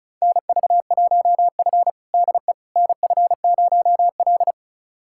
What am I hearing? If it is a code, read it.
Morse code, heard as a keyed tone, about 36 wpm, 700 Hz. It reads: NV1F DE NF0L
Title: USA Callsign DE Callsign - 4 Characters - 35wpm